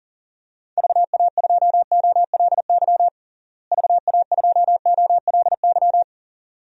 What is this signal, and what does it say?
Morse code, heard as a keyed tone, about 37 wpm, 700 Hz. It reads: VA2OLY VA2OLY
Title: DXCC-100 Call Signs - 6 Characters - Repeated 2x - 40wpm